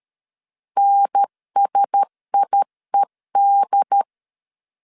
Morse code, heard as a keyed tone, about 13 wpm, 800 Hz. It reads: NSIED
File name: LEZIONI / 7 GF9Z / 7 GF9Z 12wpm 2147